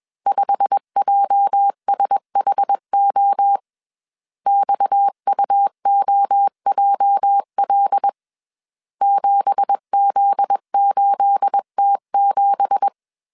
Morse code, heard as a keyed tone, about 21 wpm, 800 Hz. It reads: 5JSHO XUOJL 7Z8T7